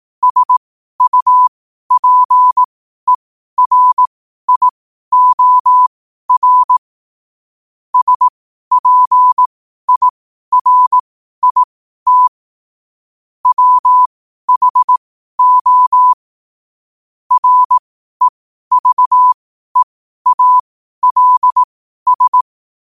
Morse code, heard as a keyed tone, about 18 wpm, spaced as for 12 wpm, 1000 Hz. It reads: SUPERIOR SPIRIT WHO REVEALS
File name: qotdfiles / 12wpm